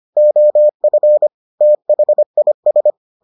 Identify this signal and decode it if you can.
Morse code, heard as a keyed tone, about 25 wpm, 600 Hz. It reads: OF THIS